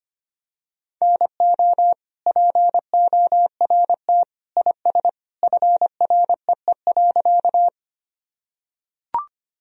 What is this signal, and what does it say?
Morse code, heard as a keyed tone, about 25 wpm, 700 Hz. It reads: NO PORT IS FREE.
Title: Sentences from Top 1000 Words - Review Part 1 - 25wpm